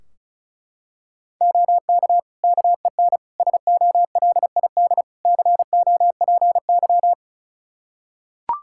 Morse code, heard as a keyed tone, about 36 wpm, 700 Hz. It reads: OK KEN SOLID COPY
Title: Practice QSO - Exchange 3 - More Info - 35wpm